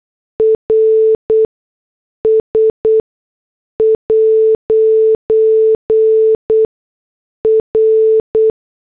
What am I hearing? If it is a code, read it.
Morse code, heard as a keyed tone, about 8 wpm, spaced as for 6 wpm, 440 Hz. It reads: RS' R